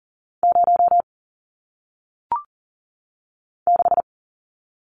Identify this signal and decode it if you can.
Morse code, heard as a keyed tone, about 41 wpm, 700 Hz. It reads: 0 6
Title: Numbers - 1-Digit 40wpm